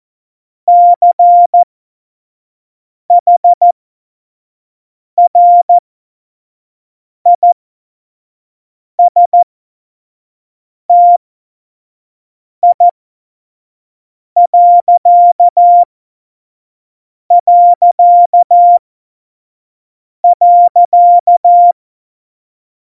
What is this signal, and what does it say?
Morse code, heard as a keyed tone, about 14 wpm, spaced as for 5 wpm, 700 Hz. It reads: CHRISTI...
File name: Morsekurs / Witze / Witz 05. Januar 2021, 25 F70